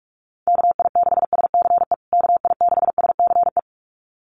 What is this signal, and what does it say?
Morse code, heard as a keyed tone, about 44 wpm, 700 Hz. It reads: KI6SCE KI6SCE
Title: USA Call Signs Repeated 2x 45wpm